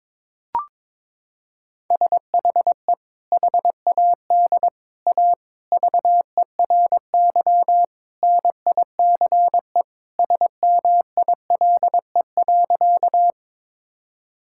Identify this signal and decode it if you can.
Morse code, heard as a keyed tone, about 22 wpm, 700 Hz. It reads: SHE HAD A VERY NICE SMILE.
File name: practice / 22wpm / Sentences with 2-Word Phrases - 22wpm